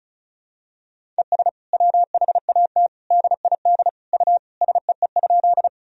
Morse code, heard as a keyed tone, about 36 wpm, 700 Hz. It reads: ES WHAT DID U SEE?
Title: Sentences from Top 100 Words and Abbreviations 35wpm